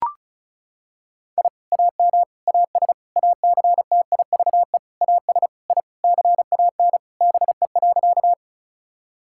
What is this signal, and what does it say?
Morse code, heard as a keyed tone, about 36 wpm, 700 Hz. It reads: I AM AS ACTIVE AS I CAN BE.
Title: Sentences from Top 1000 Words - Review Part 6 - 35wpm